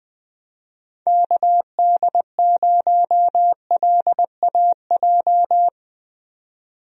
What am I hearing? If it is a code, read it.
Morse code, heard as a keyed tone, about 20 wpm, 700 Hz. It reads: KD0LAJ
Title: USA Call Signs - 6 Characters - 20wpm